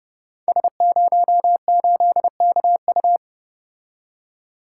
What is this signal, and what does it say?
Morse code, heard as a keyed tone, about 29 wpm, 700 Hz. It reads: S08KU
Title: Code Groups - 5 Letter:Numbers - 30wpm